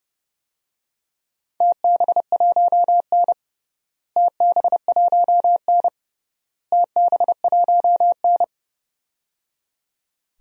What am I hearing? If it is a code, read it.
Morse code, heard as a keyed tone, about 30 wpm, 700 Hz. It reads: TB1N TB1N TB1N
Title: DXCC-100 Call Signs - 4 Characters - Repeated 3x - 30wpm